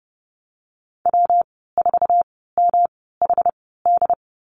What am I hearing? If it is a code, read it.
Morse code, heard as a keyed tone, about 30 wpm, spaced as for 17 wpm, 700 Hz. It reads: W4MHD
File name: Lesson 19g - Letter w - Call Signs 30wpm